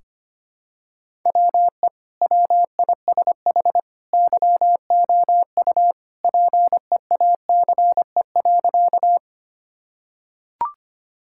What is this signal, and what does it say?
Morse code, heard as a keyed tone, about 25 wpm, 700 Hz. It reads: WE WISH YOU PEACE.
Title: Sentences from Top 1600 Words 25wpm